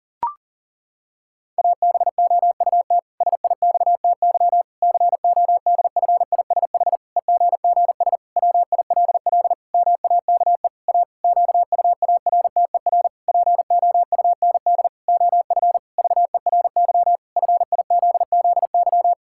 Morse code, heard as a keyed tone, about 40 wpm, 700 Hz. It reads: ABOUT SIXTY CODFISH EGGS WILL MAKE A QUARTER POUND OF VERY FIZZY
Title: Pangram Challenge v2 - 40wpm